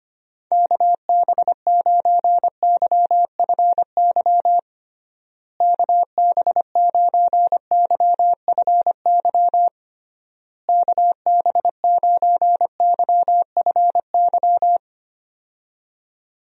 Morse code, heard as a keyed tone, about 25 wpm, 700 Hz. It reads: KB9YFY KB9YFY KB9YFY
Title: USA Call Signs - 6 Characters - Repeated 3x - 25wpm